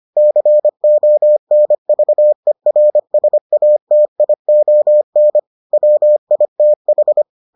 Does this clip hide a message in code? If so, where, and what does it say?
Yes: Morse code from 0.2 to 7.2 s, heard as a keyed tone, about 25 wpm, 600 Hz. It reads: CONVERSATION WITH